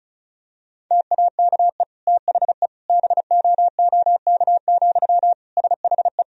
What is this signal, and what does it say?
Morse code, heard as a keyed tone, about 36 wpm, 700 Hz. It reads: TAKE THE BOOK, SHE
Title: Sentences from Top 1000 Words - Review Part 1 - 35wpm